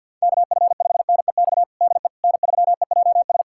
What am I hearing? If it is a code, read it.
Morse code, heard as a keyed tone, about 52 wpm, 700 Hz. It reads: KF5NEX DE N3EJS
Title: USA Callsign DE Callsign 50wpm